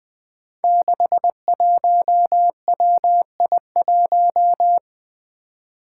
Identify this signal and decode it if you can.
Morse code, heard as a keyed tone, about 20 wpm, 700 Hz. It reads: 61WI1
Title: Five Challenging Letter-Number Combinations 20wpm